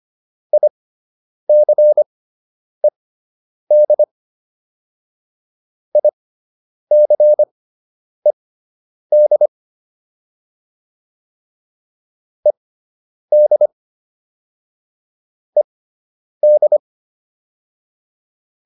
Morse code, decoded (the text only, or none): ICED ICED ED ED